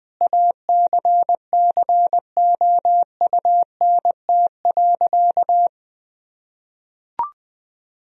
Morse code, heard as a keyed tone, about 20 wpm, 700 Hz. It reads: ACCOUNT.
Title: Sentences from Top 1000 Words - Review Part 4 - 20wpm